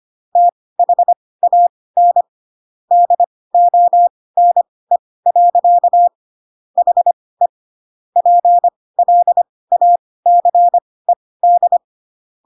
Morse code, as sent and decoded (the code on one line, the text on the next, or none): - .... .- -. / -.. --- -. .-.-.- / .... / .--. .-.. .- -.-. -..
THAN DONE. HE PLACED